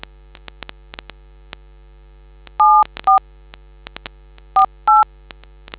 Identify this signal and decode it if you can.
DTMF, heard as telephone touch tones, tones of about 160 ms. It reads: 74 48